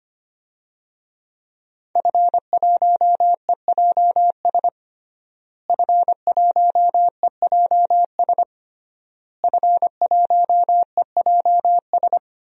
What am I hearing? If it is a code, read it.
Morse code, heard as a keyed tone, about 25 wpm, 700 Hz. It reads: F1EJS F1EJS F1EJS
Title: DXCC-100 Call Signs - 5 Characters - Repeated 3x - 25wpm